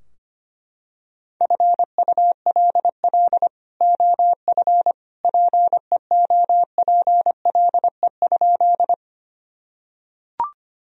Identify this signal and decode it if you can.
Morse code, heard as a keyed tone, about 25 wpm, 700 Hz. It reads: FULL OF PEOPLE?
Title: Sentences from Top 1000 Words - Review Part 3 - 25wpm